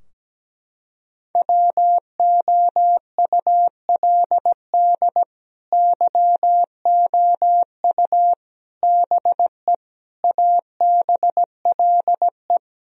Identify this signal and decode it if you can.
Morse code, heard as a keyed tone, about 17 wpm, 700 Hz. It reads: WOULD YOU BE ABLE
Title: Sentences from Top 1000 Words - Review Part 2 - 17wpm